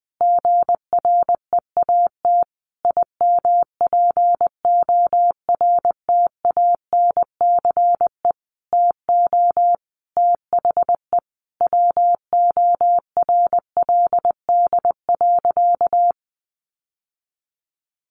Morse code, heard as a keyed tone, about 20 wpm, 700 Hz. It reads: GREAT IMPORTANCE TO THE WORLD.
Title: Sentences from Top 1200 Words 20wpm